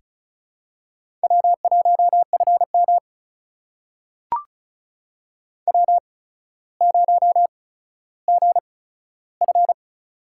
Morse code, heard as a keyed tone, about 36 wpm, spaced as for 15 wpm, 700 Hz. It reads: W1FM W0GF
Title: USA Call Signs - 4 Characters - 8x Character Spacing - 35wpm